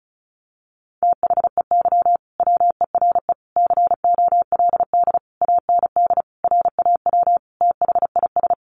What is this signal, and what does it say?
Morse code, heard as a keyed tone, about 34 wpm, 700 Hz. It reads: THEY WERE COLD AND RAW THIS
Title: Sentences from Top 1500 Words 35wpm